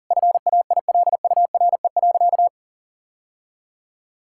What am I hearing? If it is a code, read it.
Morse code, heard as a keyed tone, about 40 wpm, 700 Hz. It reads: FAILURE.
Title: Sentences from Top 1000 Words - Review Part 6 - 40wpm